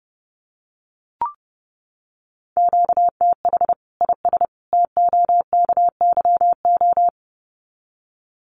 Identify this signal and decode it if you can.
Morse code, heard as a keyed tone, about 30 wpm, 700 Hz. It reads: QTH IS TOKYO
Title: Common QSO Phrases 30wpm